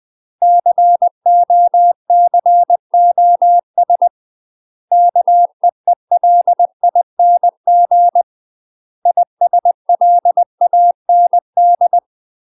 Morse code, heard as a keyed tone, about 20 wpm, 700 Hz. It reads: COCOS KEELING ISLAND
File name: UK MASTER QSOs TRAINING/20WPM